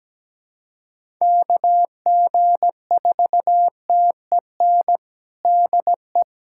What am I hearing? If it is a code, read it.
Morse code, heard as a keyed tone, about 17 wpm, 700 Hz. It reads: KG4TEN DE
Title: USA Callsign DE Callsign - 6 Characters - 17wpm